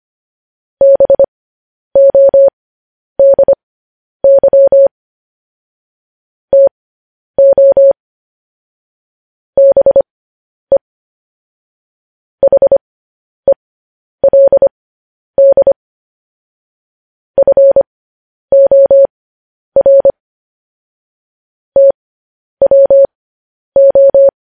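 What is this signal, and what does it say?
Morse code, heard as a keyed tone, about 25 wpm, spaced as for 10 wpm, 550 Hz. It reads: BODY TO BE HELD FOR TWO
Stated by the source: Morse code news headlines.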